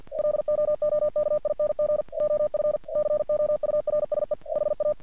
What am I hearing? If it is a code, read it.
Morse code, heard as a keyed tone, about 50 wpm, 600 Hz. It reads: LOOKING OF COURSE HA
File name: archived / CRSnet